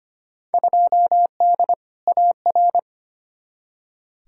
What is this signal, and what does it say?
Morse code, heard as a keyed tone, about 25 wpm, 700 Hz. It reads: 2D AR
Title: Field Day Exchanges Encore - 25wpm